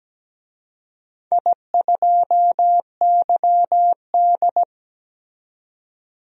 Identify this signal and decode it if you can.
Morse code, heard as a keyed tone, about 17 wpm, 700 Hz. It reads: I2YD
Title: DXCC-100 Call Signs - 4 Characters - 17wpm